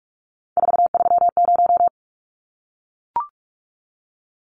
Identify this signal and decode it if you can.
Morse code, heard as a keyed tone, about 44 wpm, 700 Hz. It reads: 430
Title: Numbers - 3-Digit 45wpm